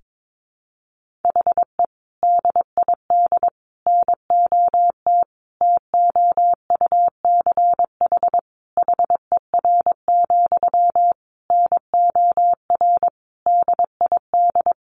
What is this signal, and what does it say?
Morse code, heard as a keyed tone, about 22 wpm, 700 Hz. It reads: HE DID NOT TOUCH HER, NOR DID
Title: Sentences from Top 1500 Words 22wpm